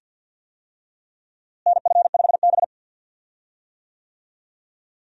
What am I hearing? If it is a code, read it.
Morse code, heard as a keyed tone, about 52 wpm, 700 Hz. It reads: NF5B
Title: USA Call Signs - 4 Characters - Repeated 2x - 50wpm